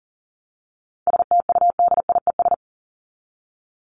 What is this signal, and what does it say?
Morse code, heard as a keyed tone, about 40 wpm, 700 Hz. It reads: STUDIES
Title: ES:S Suffix - Top 100 Words with Base Words 40wpm